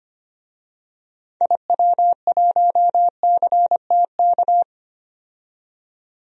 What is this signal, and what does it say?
Morse code, heard as a keyed tone, about 25 wpm, 700 Hz. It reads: IW1CTK
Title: DXCC-100 Call Signs - 6 Characters - 25wpm